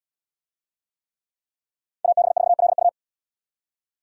Morse code, heard as a keyed tone, about 72 wpm, 700 Hz. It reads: AB4LU